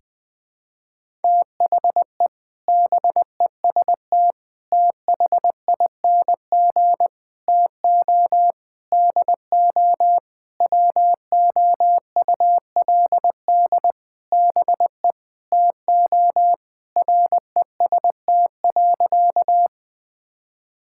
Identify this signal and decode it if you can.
Morse code, heard as a keyed tone, about 21 wpm, 700 Hz. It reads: THE BEST THING TO DO WOULD BE TO REST.
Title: Sentences with 8-Word Phrases - 20wpm